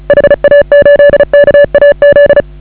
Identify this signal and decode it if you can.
Morse code, heard as a keyed tone, about 34 wpm, 560 Hz. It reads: HA8KAZ
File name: AUDIO2019 / HA8KAZ CW